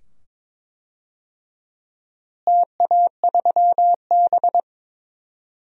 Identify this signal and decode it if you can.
Morse code, heard as a keyed tone, about 22 wpm, 700 Hz. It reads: TA3B
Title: DXCC-100 Call Signs - 4 Characters - Repeated 2x - 22wpm